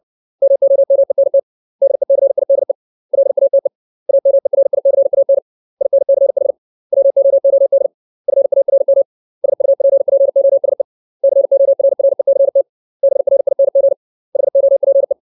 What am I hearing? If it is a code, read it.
Morse code, heard as a keyed tone, about 60 wpm, 550 Hz. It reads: MOMENT DECIDE LATE AMERICAN INCH WOOD FARM SUPPOSE CORRECT BRING HOPE
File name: 632 words / 632 Words 60WPM